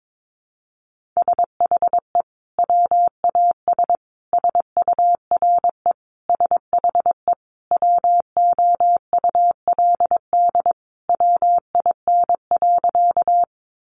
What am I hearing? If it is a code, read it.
Morse code, heard as a keyed tone, about 22 wpm, 700 Hz. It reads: SHE WAS SURE SHE WOULD WIN.
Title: Sentences from Top 1500 Words 22wpm